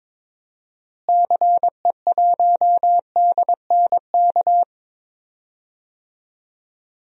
Morse code, heard as a keyed tone, about 22 wpm, 700 Hz. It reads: CE1DNK
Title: DXCC-100 Call Signs - 6 Characters - Repeated 3x - 22wpm